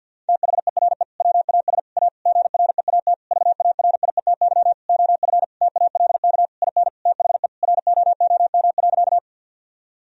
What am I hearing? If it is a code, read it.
Morse code, heard as a keyed tone, about 50 wpm, 700 Hz. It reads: THERE WAS A GREAT VARIETY OF TALK IN THE ROOM.
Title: Sentences from Top 1000 Words - Review Part 4 - 50wpm